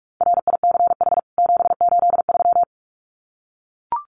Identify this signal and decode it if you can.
Morse code, heard as a keyed tone, about 44 wpm, 700 Hz. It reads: RICH 783
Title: CWT Exchange 45wpm